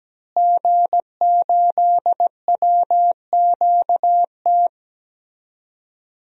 Morse code, heard as a keyed tone, about 17 wpm, 700 Hz. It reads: G8WQT
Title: DXCC-100 Call Signs - 5 Characters - Repeated 2x - 17wpm